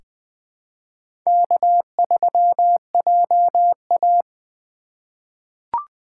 Morse code, heard as a keyed tone, about 21 wpm, 700 Hz. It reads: K3JA
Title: 2023 POTA Activator Call Signs - 20wpm